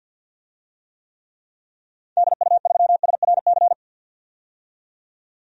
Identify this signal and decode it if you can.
Morse code, heard as a keyed tone, about 50 wpm, 700 Hz. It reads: DU3SRC
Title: DXCC-100 Call Signs - 6 Characters - 50wpm